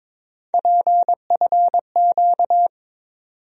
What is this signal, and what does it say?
Morse code, heard as a keyed tone, about 21 wpm, 700 Hz. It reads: PFQ
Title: USA Call Sign Postfixes - 3 Letters 22wpm